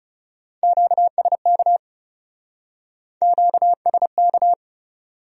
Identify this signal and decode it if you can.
Morse code, heard as a keyed tone, about 39 wpm, 700 Hz. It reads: QSK QSK